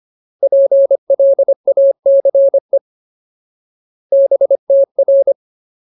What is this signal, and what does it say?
Morse code, heard as a keyed tone, about 25 wpm, 550 Hz. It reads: PLACE BTR
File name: Ultimate Word List / 25WPM Ultimate